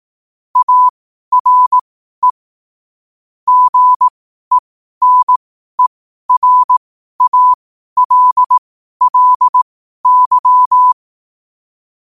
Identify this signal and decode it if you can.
Morse code, heard as a keyed tone, about 18 wpm, spaced as for 12 wpm, 1000 Hz. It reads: ARE GENERALLY
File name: qotdfiles / 12wpm